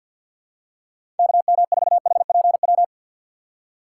Morse code, heard as a keyed tone, about 51 wpm, 700 Hz. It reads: KM4HPW
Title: USA Call Signs - 6 Characters - 6x Character Spacing - 50wpm